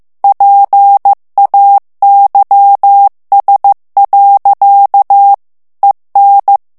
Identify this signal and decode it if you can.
Morse code, heard as a keyed tone, about 15 wpm, 800 Hz. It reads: PAYS. EN
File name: morse mp3 / FRANCAIS TEXTE CLAIR 1 VITESSE 14 MOTS MINUTES